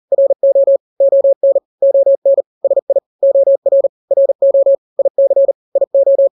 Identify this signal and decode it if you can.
Morse code, heard as a keyed tone, about 40 wpm, 550 Hz. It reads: RO ON ON SI OR RO IC IO
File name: n-grams / 2-grams 40 35 WPM